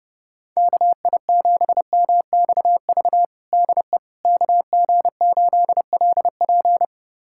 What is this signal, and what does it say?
Morse code, heard as a keyed tone, about 30 wpm, 700 Hz. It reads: KI7MXV DE KG8LP
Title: USA Callsign DE Callsign 30wpm